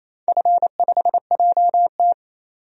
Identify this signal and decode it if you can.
Morse code, heard as a keyed tone, about 28 wpm, 700 Hz. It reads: F5JT
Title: DXCC-100 Call Signs - 4 Characters - 28wpm